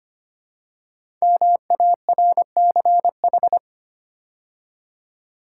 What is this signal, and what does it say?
Morse code, heard as a keyed tone, about 25 wpm, 700 Hz. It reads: MARCH
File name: 5 Letter Words - First 3 Letters Given - 25wpm